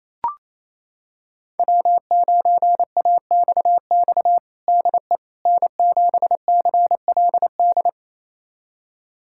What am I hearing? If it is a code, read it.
Morse code, heard as a keyed tone, about 28 wpm, 700 Hz. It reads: W9AXX DE N7CLD
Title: USA Callsign DE Callsign - 5 Characters - 28wpm